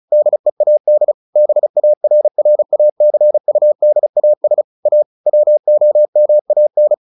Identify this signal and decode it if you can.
Morse code, heard as a keyed tone, about 34 wpm, 600 Hz. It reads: DEAD BARRACUDAS A WOMAN
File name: the old man and the sea 2 35wpm